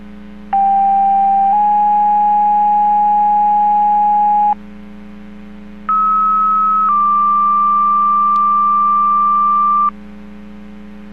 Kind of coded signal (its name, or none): two-tone page